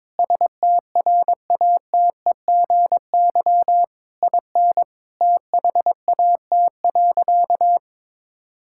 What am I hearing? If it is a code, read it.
Morse code, heard as a keyed tone, about 22 wpm, 700 Hz. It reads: STRATEGY IN THAT.
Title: Sentences from Top 1000 Words - Review Part 5 - 22wpm